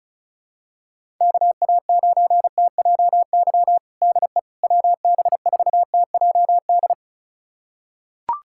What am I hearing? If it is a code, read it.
Morse code, heard as a keyed tone, about 34 wpm, 700 Hz. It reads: KA9TJY DE WB4TJD
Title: USA Callsign DE Callsign 35wpm